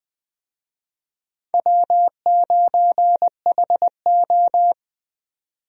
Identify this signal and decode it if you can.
Morse code, heard as a keyed tone, about 21 wpm, 700 Hz. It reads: W9HO
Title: USA Call Signs - 4 Characters - 20wpm